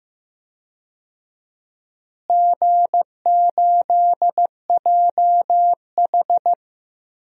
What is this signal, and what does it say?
Morse code, heard as a keyed tone, about 15 wpm, 700 Hz. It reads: G8JH
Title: DXCC-100 Call Signs - 4 Characters - Repeated 3x - 15wpm